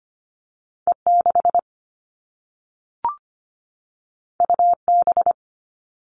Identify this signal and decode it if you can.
Morse code, heard as a keyed tone, about 25 wpm, 700 Hz. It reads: E 6 U B
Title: All Cut Numbers - 2-Digit 25wpm